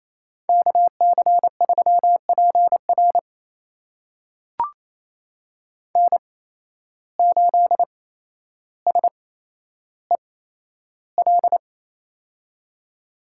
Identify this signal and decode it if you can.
Morse code, heard as a keyed tone, about 27 wpm, spaced as for 12 wpm, 700 Hz. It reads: KC3PR N8SEL
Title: USA Call Signs - 5 Characters - 8x Character Spacing - 28wpm